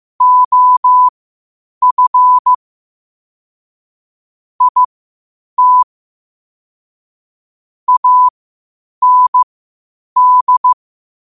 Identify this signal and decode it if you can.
Morse code, heard as a keyed tone, about 15 wpm, spaced as for 8 wpm, 1000 Hz. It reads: OF IT AND
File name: qotdfiles / 8wpm